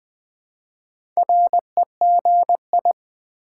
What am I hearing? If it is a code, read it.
Morse code, heard as a keyed tone, about 20 wpm, 700 Hz. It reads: REGI
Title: Build A Word - 2 Letters At A Time - 6-Letter Words - 20wpm